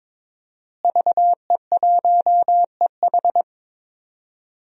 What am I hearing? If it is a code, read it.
Morse code, heard as a keyed tone, about 22 wpm, 700 Hz. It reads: VE1EH
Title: DXCC-100 Call Signs - 5 Characters - Repeated 2x - 22wpm